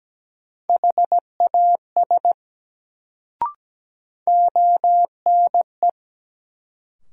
Morse code, heard as a keyed tone, about 17 wpm, 700 Hz. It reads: HAS ONE
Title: Sentences from Top 100 Words - New Word Drill - 17wpm